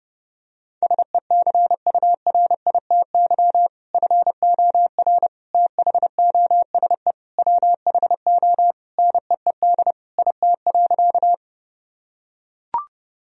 Morse code, heard as a keyed tone, about 30 wpm, 700 Hz. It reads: SECURITY FOR THOSE WHO NEED IT.
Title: Sentences from Top 1000 Words - Review Part 4 - 30wpm